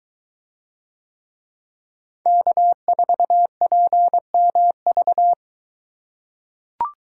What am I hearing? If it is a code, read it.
Morse code, heard as a keyed tone, about 24 wpm, 700 Hz. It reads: K4PMV